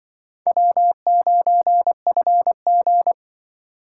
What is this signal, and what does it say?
Morse code, heard as a keyed tone, about 24 wpm, 700 Hz. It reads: W9FG
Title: Speed Racing - USA 1x2 Call Signs 20wpm